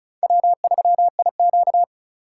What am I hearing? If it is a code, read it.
Morse code, heard as a keyed tone, about 34 wpm, 700 Hz. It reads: W3IQ
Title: USA Call Signs - 4 Characters - 4x Character Spacing - 35wpm